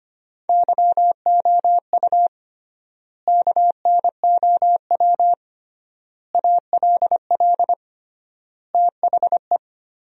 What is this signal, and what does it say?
Morse code, heard as a keyed tone, about 25 wpm, 700 Hz. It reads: YOU KNOW ALL THE
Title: 3x Word-Spacing - Sentences from Top 300 Words - 25wpm